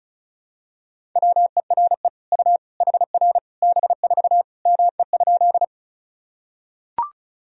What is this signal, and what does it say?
Morse code, heard as a keyed tone, about 36 wpm, 700 Hz. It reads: WERE U HR B4 ME?
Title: Sentences from Top 200 Words and Abbreviations 35wpm